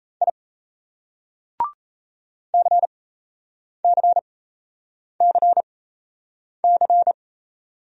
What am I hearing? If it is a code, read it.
Morse code, heard as a keyed tone, about 40 wpm, spaced as for 8 wpm, 700 Hz. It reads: I CCCC